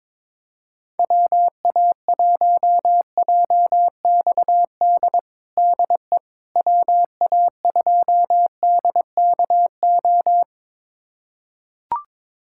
Morse code, heard as a keyed tone, about 22 wpm, 700 Hz. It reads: WA1JXD DE WA2DKO
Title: USA Callsign DE Callsign - 6 Characters - 22wpm